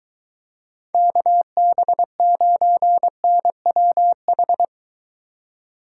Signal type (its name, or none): Morse code